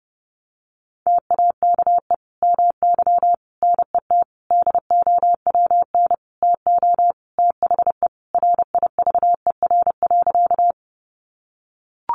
Morse code, heard as a keyed tone, about 30 wpm, 700 Hz. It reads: TAKE MY NET DOWN TO THE RIVER.
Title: Sentences from Top 1400 Words 30wpm